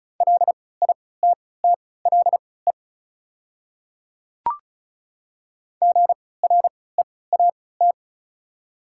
Morse code, heard as a keyed tone, about 36 wpm, spaced as for 20 wpm, 700 Hz. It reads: LITTLE GREAT
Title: 3x Character Spacing - Top 200 Words - 35wpm